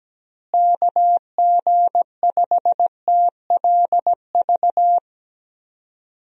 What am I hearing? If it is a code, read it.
Morse code, heard as a keyed tone, about 17 wpm, 700 Hz. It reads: KG5TLV
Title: USA Call Signs - 6 Characters - Repeated 3x - 17wpm